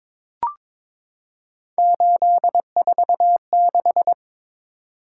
Morse code, heard as a keyed tone, about 22 wpm, 700 Hz. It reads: 846
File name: practice / 22wpm / Three Letter-Number Combinations - 22wpm